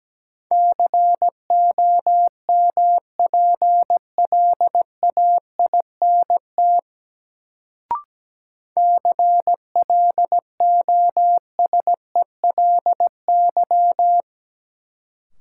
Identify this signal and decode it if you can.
Morse code, heard as a keyed tone, about 17 wpm, 700 Hz. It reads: COMPLAINT CLOSELY
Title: Sentences from Top 1400 Words - New Word Drill Alt - 17wpm